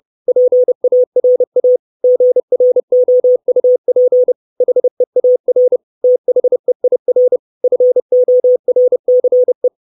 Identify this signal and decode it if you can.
Morse code, heard as a keyed tone, about 30 wpm, 500 Hz. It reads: PARA GROUP HEAR THEIR FORCE
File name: 632 Words 30WPM